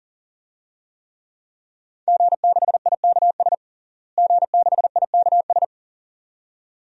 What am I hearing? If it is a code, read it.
Morse code, heard as a keyed tone, about 40 wpm, 700 Hz. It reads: G6IKS G6IKS
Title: DXCC-100 Call Signs - 5 Characters - Repeated 2x - 40wpm